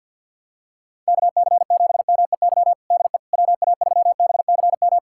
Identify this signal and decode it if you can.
Morse code, heard as a keyed tone, about 49 wpm, 700 Hz. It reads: KC7MEY DE WA3BCM
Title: USA Callsign DE Callsign 50wpm